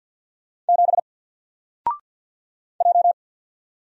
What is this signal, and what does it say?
Morse code, heard as a keyed tone, about 53 wpm, 700 Hz. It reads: 7 J
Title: Single Letter-Number 50wpm